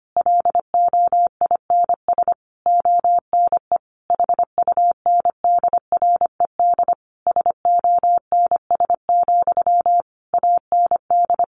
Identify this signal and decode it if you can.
Morse code, heard as a keyed tone, about 25 wpm, 700 Hz. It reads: LOINS ONE HUNDRED SONS, AND
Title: A Sound Of Thunder 25wpm